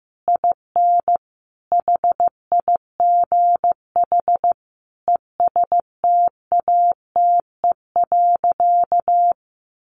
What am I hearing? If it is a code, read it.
Morse code, heard as a keyed tone, about 15 wpm, 700 Hz. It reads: IN HIGH ESTATE.
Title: Sentences from Top 1200 Words 15wpm